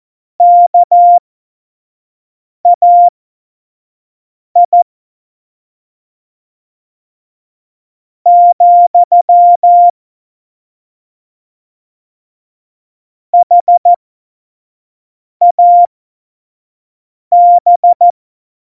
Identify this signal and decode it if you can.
Morse code, heard as a keyed tone, about 14 wpm, spaced as for 4 wpm, 700 Hz. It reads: KAI, HAB
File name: Morsekurs / Witze / Witz 26. Februar 2019, 25 F70